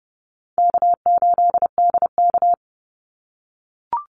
Morse code, heard as a keyed tone, about 30 wpm, 700 Hz. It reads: K8DK
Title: Easy Call Signs - 30wpm